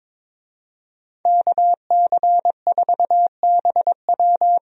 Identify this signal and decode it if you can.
Morse code, heard as a keyed tone, about 21 wpm, 700 Hz. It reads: KC4BW
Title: USA Call Signs - 5 Characters - Repeated 3x - 22wpm